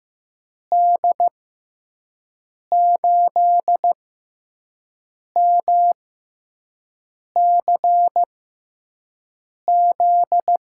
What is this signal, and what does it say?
Morse code, heard as a keyed tone, about 15 wpm, spaced as for 5 wpm, 700 Hz. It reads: D8MCZ
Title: USA Call Signs - 6 Characters - 6x Character Spacing - 15wpm